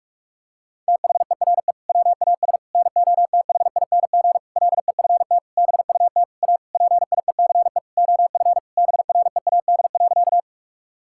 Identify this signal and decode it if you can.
Morse code, heard as a keyed tone, about 41 wpm, 700 Hz. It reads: THERE WAS NOTHING LEFT BUT A PIECE OF BREAD.
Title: Sentences from Top 1600 Words 45wpm